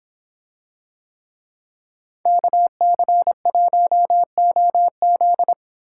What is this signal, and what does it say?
Morse code, heard as a keyed tone, about 25 wpm, 700 Hz. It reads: KC1OZ